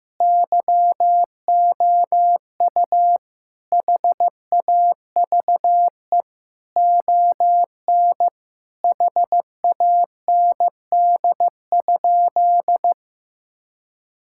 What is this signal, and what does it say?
Morse code, heard as a keyed tone, about 15 wpm, 700 Hz. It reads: YOU HAVE ON HAND?
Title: Sentences from Top 1000 Words - Review Part 5 - 15wpm